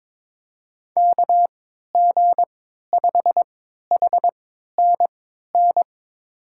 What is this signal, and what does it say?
Morse code, heard as a keyed tone, about 23 wpm, spaced as for 13 wpm, 700 Hz. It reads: KG5HNN
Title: USA Call Signs - 6 Characters - 3x Character Spacing - 22wpm